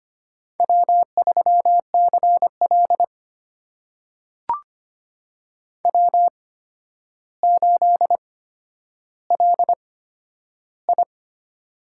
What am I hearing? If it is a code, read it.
Morse code, heard as a keyed tone, about 25 wpm, spaced as for 11 wpm, 700 Hz. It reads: W3CL W8LI
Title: USA Call Signs - 4 Characters - 8x Character Spacing - 25wpm